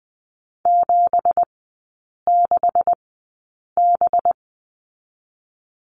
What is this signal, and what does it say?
Morse code, heard as a keyed tone, about 20 wpm, spaced as for 8 wpm, 700 Hz. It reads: 76B
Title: 184 Lesson 30e - Letter b - Three Characters 20wpm